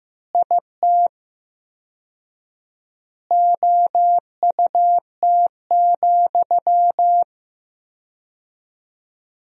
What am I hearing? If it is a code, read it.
Morse code, heard as a keyed tone, about 15 wpm, 700 Hz. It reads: IT OUT,
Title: 4x Word-Spacing - Sentences from Top 300 Words - 15wpm